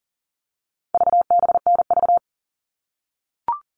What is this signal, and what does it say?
Morse code, heard as a keyed tone, about 38 wpm, 700 Hz. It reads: VBNV
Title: All Cut Numbers - 4-Digit 40wpm